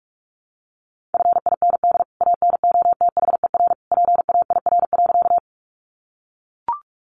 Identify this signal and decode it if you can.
Morse code, heard as a keyed tone, about 44 wpm, 700 Hz. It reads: FIND ANOTHER PAIR.
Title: Sentences from Top 1200 Words 45wpm